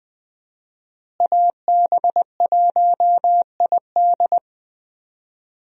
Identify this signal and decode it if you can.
Morse code, heard as a keyed tone, about 20 wpm, 700 Hz. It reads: AB1ID
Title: USA Call Signs - 5 Characters - Repeated 3x - 20wpm